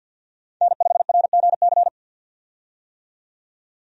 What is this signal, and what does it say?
Morse code, heard as a keyed tone, about 51 wpm, 700 Hz. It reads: N5RGC
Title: USA Call Signs - 5 Characters - 50wpm